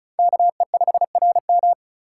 Morse code, heard as a keyed tone, about 46 wpm, 700 Hz. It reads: KE5RM